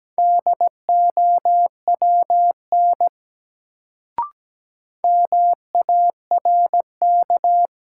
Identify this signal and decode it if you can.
Morse code, heard as a keyed tone, about 17 wpm, 700 Hz. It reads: DOWN MARK
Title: Sentences from Top 300 Words - New Word Drill - 17wpm